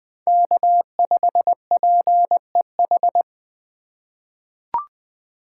Morse code, heard as a keyed tone, about 20 wpm, 700 Hz. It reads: K5PEH
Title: USA Call Signs - 2x Character Spacing - 20wpm